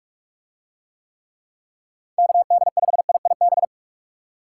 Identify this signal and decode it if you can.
Morse code, heard as a keyed tone, about 46 wpm, 700 Hz. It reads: KD5IIB